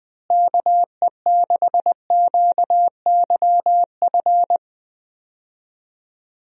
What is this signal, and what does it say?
Morse code, heard as a keyed tone, about 21 wpm, 700 Hz. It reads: KE6QYF
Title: USA Call Signs - 6 Characters - Repeated 3x - 20wpm